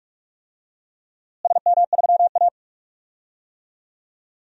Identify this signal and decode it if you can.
Morse code, heard as a keyed tone, about 46 wpm, 700 Hz. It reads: SM3A
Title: DXCC-100 Call Signs - 4 Characters - Repeated 2x - 45wpm